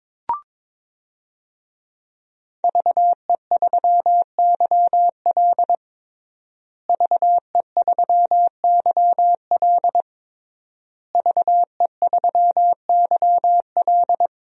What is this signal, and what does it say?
Morse code, heard as a keyed tone, about 22 wpm, 700 Hz. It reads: VE3YL VE3YL VE3YL
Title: DXCC-100 Call Signs - 5 Characters - Repeated 3x - 22wpm